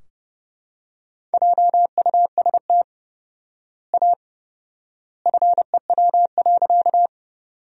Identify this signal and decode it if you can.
Morse code, heard as a keyed tone, about 30 wpm, 700 Hz. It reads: JUST A FEW.
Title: 4x Word-Spacing - Sentences from Top 300 Words - 30wpm